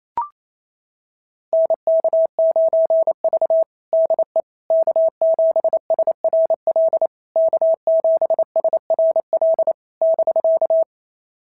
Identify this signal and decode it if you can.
Morse code, heard as a keyed tone, about 27 wpm, 650 Hz. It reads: NK9V DE K7SRL K7SRL <BK>
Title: SOTA QSO Practice 28wpm